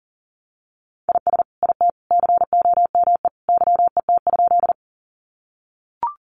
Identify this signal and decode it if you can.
Morse code, heard as a keyed tone, about 40 wpm, 700 Hz. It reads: IS IT COME YET?
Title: Sentences from Top 500 Words 40wpm